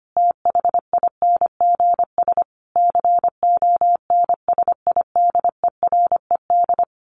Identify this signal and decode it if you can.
Morse code, heard as a keyed tone, about 25 wpm, 700 Hz. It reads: THINGS CONSIDERED
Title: Sets of 5 Words 25wpm